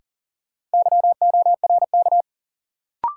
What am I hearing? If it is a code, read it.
Morse code, heard as a keyed tone, about 40 wpm, 700 Hz. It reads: YORK
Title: Sets of 2 Words - New Word Drill - 40wpm